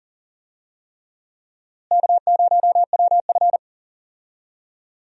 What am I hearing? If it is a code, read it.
Morse code, heard as a keyed tone, about 40 wpm, 700 Hz. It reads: K0WF
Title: USA Call Signs - 4 Characters - Repeated 3x - 40wpm